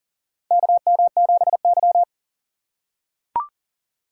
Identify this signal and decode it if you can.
Morse code, heard as a keyed tone, about 39 wpm, 700 Hz. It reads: KM7Y
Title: USA Call Signs - 4 Characters - 6x Character Spacing - 40wpm